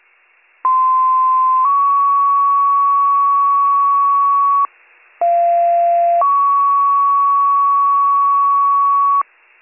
Two-tone page, heard as a one-second tone, then a three-second tone, about 1040 then 1120 Hz, about 690 then 1120 Hz. It is heard from 0.6 to 9.2 s.